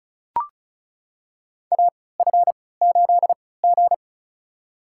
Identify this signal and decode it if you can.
Morse code, heard as a keyed tone, about 36 wpm, spaced as for 20 wpm, 700 Hz. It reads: AF8G
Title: USA Call Signs - 3x Character Spacing - 35wpm